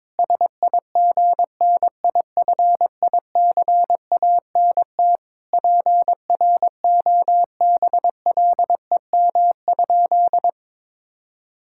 Morse code, heard as a keyed tone, about 22 wpm, 700 Hz. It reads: SIGNIFICANT PROBLEM?
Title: Sentences from Top 1000 Words - Review Part 6 - 22wpm